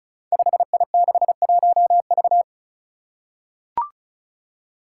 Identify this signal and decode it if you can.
Morse code, heard as a keyed tone, about 34 wpm, 700 Hz. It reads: 5I61V
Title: Five Challenging Letter-Number Combinations 35wpm